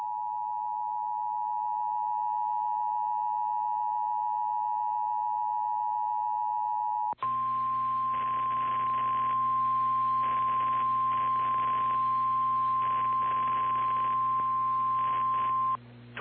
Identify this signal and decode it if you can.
Emergency Alert System (SAME) tones (the 853 and 960 Hz attention tone), with noise. It is heard from 0.7 to 15.7 s.